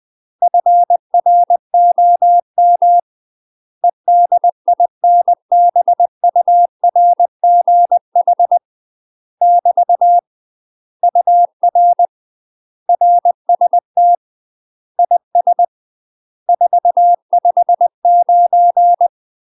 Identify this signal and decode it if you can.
Morse code, heard as a keyed tone, about 20 wpm, 700 Hz. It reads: FROM EDINBURGH <BT> UR RST IS 459